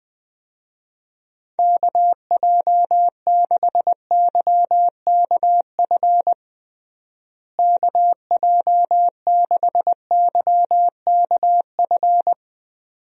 Morse code, heard as a keyed tone, about 21 wpm, 700 Hz. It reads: KJ6YKF KJ6YKF
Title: USA Call Signs - 6 Characters - Repeated 2x - 20wpm